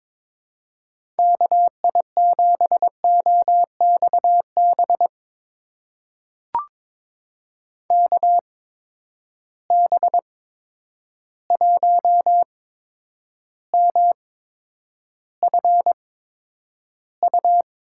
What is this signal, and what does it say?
Morse code, heard as a keyed tone, about 22 wpm, spaced as for 9 wpm, 700 Hz. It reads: KI7OXB KB1MFU
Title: USA Call Signs - 6 Characters - 8x Character Spacing - 22wpm